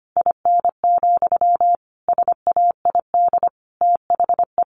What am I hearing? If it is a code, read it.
Morse code, heard as a keyed tone, about 25 wpm, 700 Hz. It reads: IN, SAID THE
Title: Sentences from Top 1000 Words 25wpm